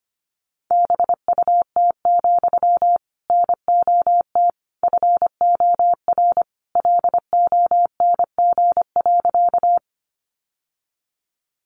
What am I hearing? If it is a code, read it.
Morse code, heard as a keyed tone, about 25 wpm, 700 Hz. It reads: BUT, NOT FOR LONG.
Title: Sentences from Top 600 Words 25wpm